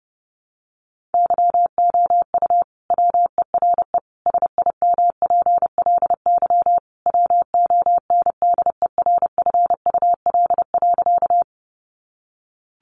Morse code, heard as a keyed tone, about 30 wpm, 700 Hz. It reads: YOU WERE SIMPLY WONDERFUL.
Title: Sentences from Top 1100 Words 30wpm